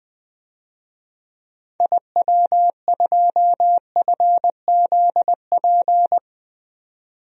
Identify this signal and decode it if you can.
Morse code, heard as a keyed tone, about 20 wpm, 700 Hz. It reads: IW2FZP
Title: DXCC-100 Call Signs - 6 Characters - 20wpm